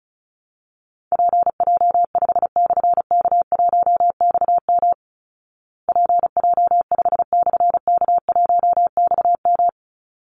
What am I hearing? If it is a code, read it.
Morse code, heard as a keyed tone, about 34 wpm, 700 Hz. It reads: PJ5/K1XM PJ5/K1XM
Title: Call Signs Repeated 3x - 35wpm